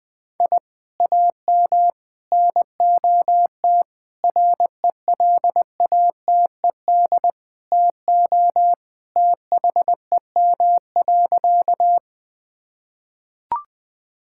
Morse code, heard as a keyed tone, about 20 wpm, 700 Hz. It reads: I AM NOT RELATED TO THEM.
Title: Sentences from Top 1000 Words - Review Part 5 - 20wpm